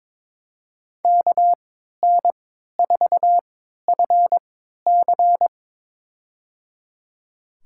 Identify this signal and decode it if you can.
Morse code, heard as a keyed tone, about 22 wpm, spaced as for 13 wpm, 700 Hz. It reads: KN4FC
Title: USA Call Signs - 5 Characters - 3x Character Spacing - 22wpm